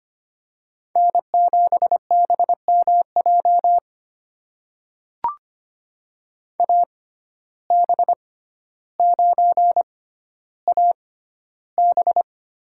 Morse code, heard as a keyed tone, about 25 wpm, spaced as for 13 wpm, 700 Hz. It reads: N7BMJ AB9AB
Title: USA Call Signs - 5 Characters - 6x Character Spacing - 25wpm